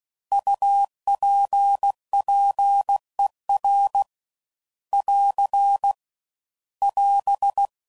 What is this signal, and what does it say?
Morse code, heard as a keyed tone, about 16 wpm, 800 Hz. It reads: UPPER <AR> <AS>
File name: Prefixes / By First Letter / Us